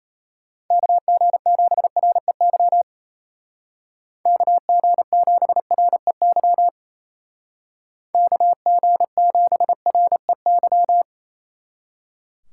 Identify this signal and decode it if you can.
Morse code, heard as a keyed tone, about 36 wpm, 700 Hz. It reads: KG7REY KG7REY KG7REY